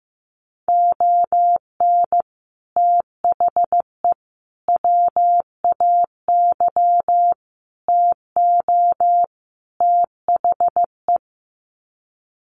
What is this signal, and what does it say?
Morse code, heard as a keyed tone, about 15 wpm, 700 Hz. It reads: ON THE WAY TO THE
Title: Sets of 5 Words 15wpm